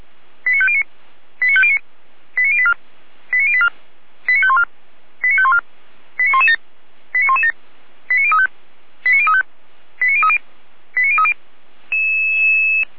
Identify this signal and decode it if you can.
Five-tone selective call (ZVEI), heard as quick runs of five tones, tones of about 70 ms. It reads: 89509 89509 89084 89084 89514 89514 89117 89117 89035 89035 89030 89030
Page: SDR/ZVEI